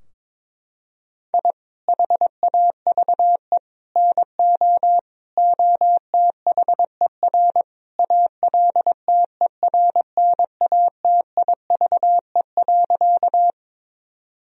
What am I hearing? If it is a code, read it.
Morse code, heard as a keyed tone, about 22 wpm, 700 Hz. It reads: I HAVE NO OTHER ALTERNATIVE.
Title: Sentences from Top 1000 Words - Review Part 4 - 22wpm